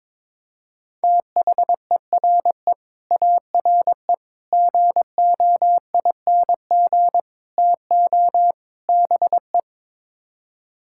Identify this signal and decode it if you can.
Morse code, heard as a keyed tone, about 22 wpm, 700 Hz. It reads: THERE ARE GOING TO BE
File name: practice / 22wpm / Build A Phrase - 5-Word Phrases - 22wpm